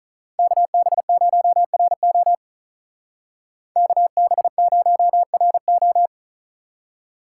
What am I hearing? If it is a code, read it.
Morse code, heard as a keyed tone, about 43 wpm, 700 Hz. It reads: KB0RO KB0RO
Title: Speed Racing - USA 2x2 Call Signs 35wpm